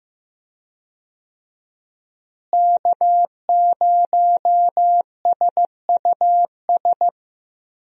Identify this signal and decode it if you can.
Morse code, heard as a keyed tone, about 15 wpm, 700 Hz. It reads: K0SUS
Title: USA Call Signs - 5 Characters - Repeated 2x - 15wpm